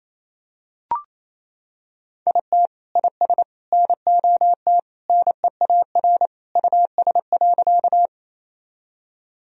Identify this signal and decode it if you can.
Morse code, heard as a keyed tone, about 28 wpm, 700 Hz. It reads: IT IS NOT NEAR US.
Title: Sentences from Top 1000 Words - Review Part 1 - 28wpm